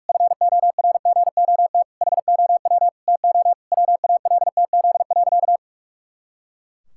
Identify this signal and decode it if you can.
Morse code, heard as a keyed tone, about 48 wpm, 700 Hz. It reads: FORGOT HOW TO WALTZ.
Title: Pangram Challenge v2 - 45wpm